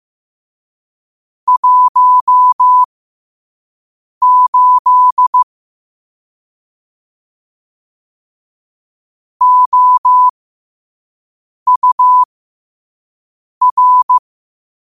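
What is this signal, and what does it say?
Morse code, heard as a keyed tone, about 15 wpm, spaced as for 5 wpm, 1000 Hz. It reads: 18 OUR
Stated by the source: Morse code sent at 5 wpm.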